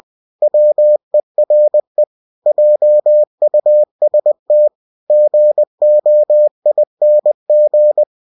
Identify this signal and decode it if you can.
Morse code, heard as a keyed tone, about 20 wpm, 600 Hz. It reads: WERE JUST GOING
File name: cw-books / winnie-the-pooh / winnie-20wpm-page-0066